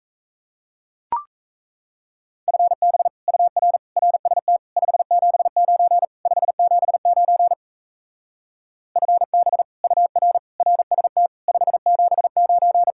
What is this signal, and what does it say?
Morse code, heard as a keyed tone, about 38 wpm, 700 Hz. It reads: FB UR RST 579 579 FB UR RST 579